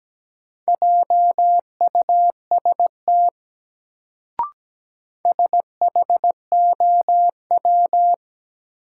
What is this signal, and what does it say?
Morse code, heard as a keyed tone, about 17 wpm, 700 Hz. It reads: JUST SHOW
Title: Sentences from Top 200 Words - New Word Drill - 17wpm